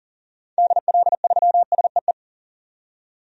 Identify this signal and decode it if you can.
Morse code, heard as a keyed tone, about 40 wpm, 700 Hz. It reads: DL3SEE
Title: DXCC-100 Call Signs - 6 Characters - 40wpm